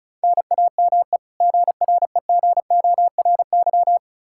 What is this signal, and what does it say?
Morse code, heard as a keyed tone, about 36 wpm, 700 Hz. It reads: NAME GREGORY
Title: Practice QSO - Exchange 1 - Beginning - 35wpm